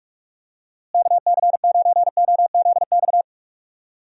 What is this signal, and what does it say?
Morse code, heard as a keyed tone, about 46 wpm, 700 Hz. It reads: KC9OZX
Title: USA Call Signs - 6 Characters - Repeated 2x - 45wpm